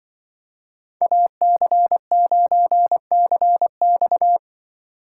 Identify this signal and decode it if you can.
Morse code, heard as a keyed tone, about 24 wpm, 700 Hz. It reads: AC9CX
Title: Speed Racing - USA 2x2 Call Signs 20wpm